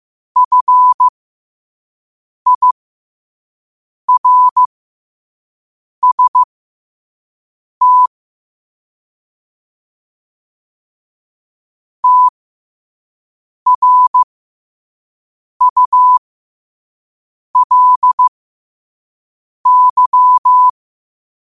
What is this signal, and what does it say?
Morse code, heard as a keyed tone, about 15 wpm, spaced as for 5 wpm, 1000 Hz. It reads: FIRST TRULY
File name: qotdfiles / 5wpm